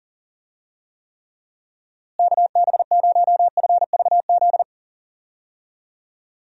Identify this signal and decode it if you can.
Morse code, heard as a keyed tone, about 40 wpm, 700 Hz. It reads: KB0FVZ
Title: USA Call Signs - 6 Characters - Repeated 3x - 40wpm